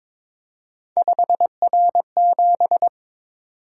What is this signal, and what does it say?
Morse code, heard as a keyed tone, about 22 wpm, 700 Hz. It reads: 5R7
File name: practice / 22wpm / Three Letter-Number Combinations - 22wpm